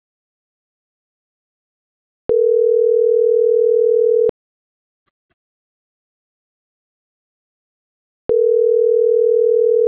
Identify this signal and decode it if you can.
Ringback tone (North America), 440 + 480 Hz.